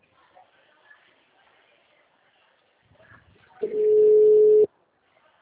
Ringback tone (Europe, 425 Hz).
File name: out- -2010- - - .31